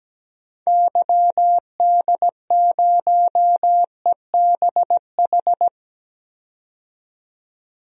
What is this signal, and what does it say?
Morse code, heard as a keyed tone, about 17 wpm, 700 Hz. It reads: YD0EBH
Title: DXCC-100 Call Signs - 6 Characters - 17wpm